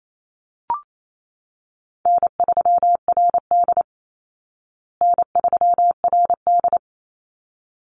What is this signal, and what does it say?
Morse code, heard as a keyed tone, about 28 wpm, 700 Hz. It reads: N3RD N3RD
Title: Call Signs Repeated 3x - 28wpm